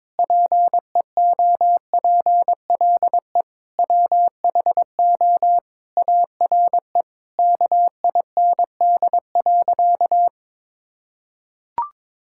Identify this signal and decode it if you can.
Morse code, heard as a keyed tone, about 22 wpm, 700 Hz. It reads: PEOPLE WHO ARE KIND.
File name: practice / 22wpm / Sentences with 8-Word Phrases - 22wpm